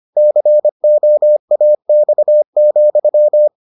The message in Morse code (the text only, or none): COAX,